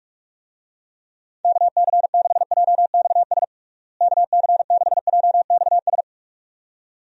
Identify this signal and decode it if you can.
Morse code, heard as a keyed tone, about 46 wpm, 700 Hz. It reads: KC6JXS KC6JXS
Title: USA Call Signs - 6 Characters - Repeated 2x - 45wpm